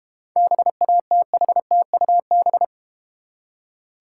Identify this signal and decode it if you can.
Morse code, heard as a keyed tone, about 32 wpm, 700 Hz. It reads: BATHTUB